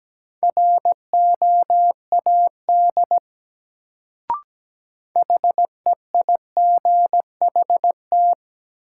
Morse code, heard as a keyed tone, about 17 wpm, 700 Hz. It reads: ROAD HEIGHT
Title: Sets of 5 Words - New Word Drill - 17wpm